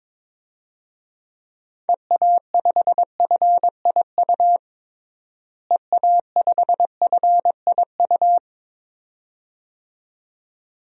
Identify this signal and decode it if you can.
Morse code, heard as a keyed tone, about 23 wpm, 700 Hz. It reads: EA5FIU EA5FIU
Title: DXCC-100 Call Signs - 6 Characters - Repeated 2x - 22wpm